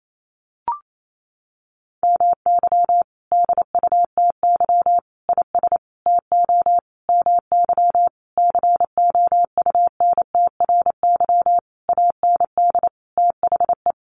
Morse code, heard as a keyed tone, about 28 wpm, 700 Hz. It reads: MY DUTY IS TO MY COUNTRY AND THE
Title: Sentences from Top 1500 Words 28wpm